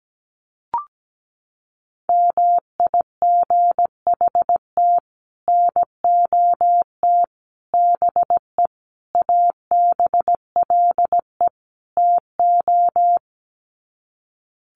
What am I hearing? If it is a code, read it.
Morse code, heard as a keyed tone, about 17 wpm, 700 Hz. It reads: MIGHT NOT BE ABLE TO
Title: Sets of 5 Words Encore2 17wpm